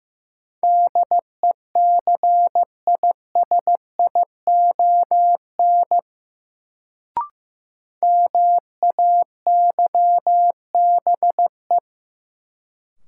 Morse code, heard as a keyed tone, about 15 wpm, 700 Hz. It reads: DECISION MAYBE
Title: Sentences from Top 1000 Words - New Word Drill - 15wpm